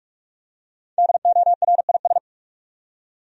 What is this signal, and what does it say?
Morse code, heard as a keyed tone, about 45 wpm, 700 Hz. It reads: DORIS
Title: USA Common Names - Max 5 Letters - 45wpm